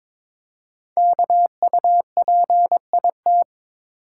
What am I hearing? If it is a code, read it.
Morse code, heard as a keyed tone, about 22 wpm, 700 Hz. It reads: KUPIT
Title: Code Groups - 5 Letter:Numbers - 22wpm